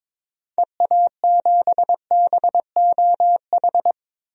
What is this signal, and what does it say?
Morse code, heard as a keyed tone, about 22 wpm, 700 Hz. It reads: EA7BOH
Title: DXCC-100 Call Signs - 6 Characters - Repeated 3x - 22wpm